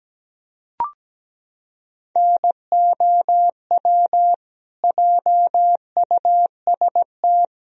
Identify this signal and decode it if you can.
Morse code, heard as a keyed tone, about 17 wpm, 700 Hz. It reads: NOW JUST
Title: Sentences from Top 1000 Words - Review Part 4 - 17wpm